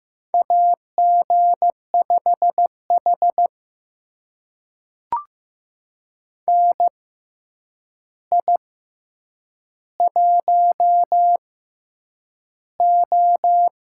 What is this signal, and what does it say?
Morse code, heard as a keyed tone, about 15 wpm, spaced as for 8 wpm, 700 Hz. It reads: AG5H NI1O
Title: USA Call Signs - 4 Characters - 6x Character Spacing - 15wpm